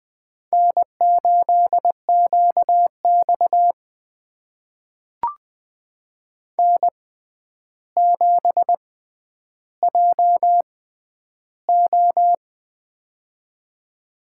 Morse code, heard as a keyed tone, about 20 wpm, spaced as for 10 wpm, 700 Hz. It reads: N8QX N7JO
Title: USA Call Signs - 4 Characters - 6x Character Spacing - 20wpm